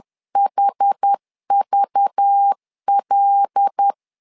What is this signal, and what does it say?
Morse code, heard as a keyed tone, about 10 wpm, 800 Hz. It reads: HVL
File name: LEZIONI / 5 LBVC / 5 LBV 10wpm 2248